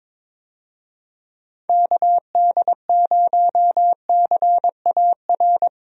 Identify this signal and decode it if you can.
Morse code, heard as a keyed tone, about 22 wpm, 700 Hz. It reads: KD0CAR
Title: USA Call Signs - 6 Characters - 22wpm